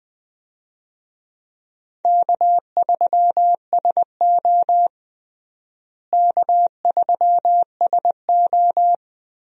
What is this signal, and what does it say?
Morse code, heard as a keyed tone, about 20 wpm, 700 Hz. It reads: K3SO K3SO
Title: USA Call Signs - 4 Characters - Repeated 3x - 20wpm